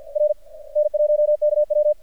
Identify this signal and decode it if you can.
Morse code, heard as a keyed tone, about 47 wpm, 590 Hz. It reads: R N1KW
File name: archived / CRSnet